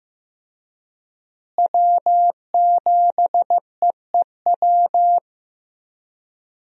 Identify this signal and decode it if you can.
Morse code, heard as a keyed tone, about 15 wpm, 700 Hz. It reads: W7EEW
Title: USA Call Signs - 5 Characters - Repeated 2x - 15wpm